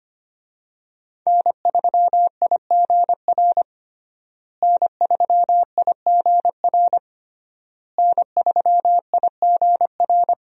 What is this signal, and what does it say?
Morse code, heard as a keyed tone, about 25 wpm, 700 Hz. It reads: N3IGR N3IGR N3IGR
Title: USA Call Signs - 5 Characters - Repeated 3x - 25wpm